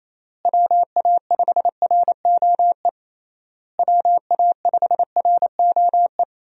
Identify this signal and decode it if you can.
Morse code, heard as a keyed tone, about 27 wpm, 700 Hz. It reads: WA5ROE WA5ROE
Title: USA Call Signs - 6 Characters - Repeated 3x - 28wpm